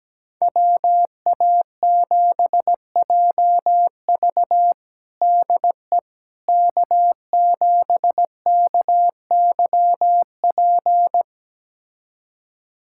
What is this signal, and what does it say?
Morse code, heard as a keyed tone, about 17 wpm, 700 Hz. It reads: WA7JV DE K7KYP
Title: USA Callsign DE Callsign - 5 Characters - 17wpm